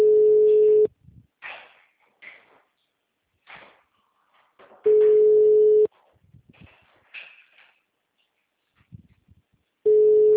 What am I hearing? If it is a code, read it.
Ringback tone (Europe, 425 Hz).